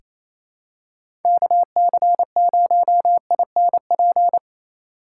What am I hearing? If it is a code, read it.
Morse code, heard as a keyed tone, about 28 wpm, 700 Hz. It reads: KC0INP
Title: K1USN SST QSO Practice - 28wpm